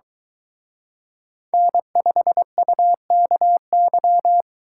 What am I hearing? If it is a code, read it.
Morse code, heard as a keyed tone, about 23 wpm, 700 Hz. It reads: N5UKY